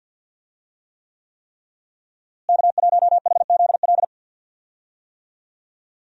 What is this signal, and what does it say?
Morse code, heard as a keyed tone, about 51 wpm, 700 Hz. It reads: K1HZL
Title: USA Call Signs - 5 Characters - Repeated 3x - 50wpm